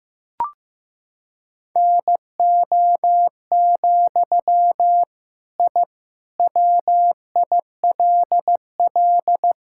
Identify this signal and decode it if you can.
Morse code, heard as a keyed tone, about 16 wpm, 700 Hz. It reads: NO, I WILL
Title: Sentences from Top 1000 Words - Review Part 6 - 15wpm